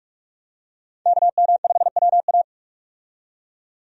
Morse code, heard as a keyed tone, about 41 wpm, 700 Hz. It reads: KM5WA
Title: USA Call Signs - 5 Characters - 45wpm